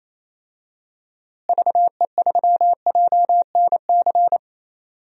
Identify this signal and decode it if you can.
Morse code, heard as a keyed tone, about 27 wpm, 700 Hz. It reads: VE3JNC
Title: DXCC-100 Call Signs - 6 Characters - Repeated 2x - 28wpm